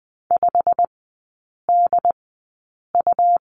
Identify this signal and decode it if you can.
Morse code, heard as a keyed tone, about 20 wpm, spaced as for 8 wpm, 700 Hz. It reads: 5DU
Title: 127 Lesson 21e - Number 6 - Three Characters 20wpm